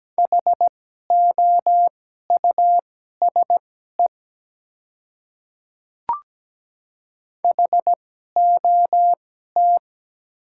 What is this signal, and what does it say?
Morse code, heard as a keyed tone, about 17 wpm, spaced as for 12 wpm, 690 Hz. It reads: HOUSE HOT
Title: 2x Character Spacing - Top 300 Words - 17wpm